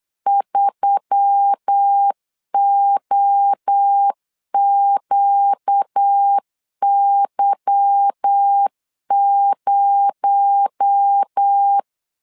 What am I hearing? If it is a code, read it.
Morse code, heard as a keyed tone, about 8 wpm, 800 Hz. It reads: 3OQY0
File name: CW / DET / 1 9 1500C S C 8wpm 5653